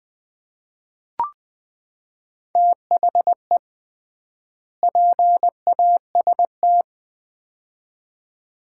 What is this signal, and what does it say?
Morse code, heard as a keyed tone, about 20 wpm, 700 Hz. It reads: THE PAST
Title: 3x Word-Spacing - Sets of 2 Words - 20wpm